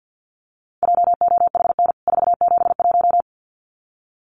Morse code, heard as a keyed tone, about 50 wpm, 700 Hz. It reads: JOHN 471
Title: CWT Exchange 50wpm